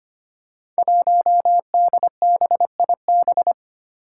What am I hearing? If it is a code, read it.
Morse code, heard as a keyed tone, about 25 wpm, 700 Hz. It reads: 1DBIB
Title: Five Challenging Letter-Number Combinations 25wpm